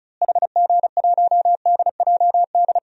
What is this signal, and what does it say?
Morse code, heard as a keyed tone, about 34 wpm, 700 Hz. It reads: HG1DJD
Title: DXCC-100 Call Signs - 6 Characters - Repeated 3x - 35wpm